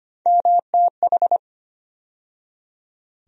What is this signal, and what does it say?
Morse code, heard as a keyed tone, about 25 wpm, 700 Hz. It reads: MTH
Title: USA Call Sign Postfixes - 3 Letters 25wpm